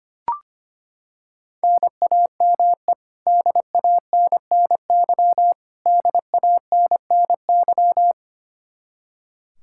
Morse code, heard as a keyed tone, about 25 wpm, 700 Hz. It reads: NAME DANNY DANNY
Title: Practice QSO - Exchange 1 - Beginning - 25wpm